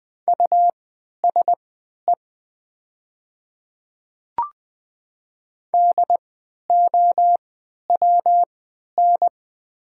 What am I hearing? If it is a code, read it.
Morse code, heard as a keyed tone, about 20 wpm, spaced as for 11 wpm, 700 Hz. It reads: USE DOWN
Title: 3x Character Spacing - Top 200 Words - 20wpm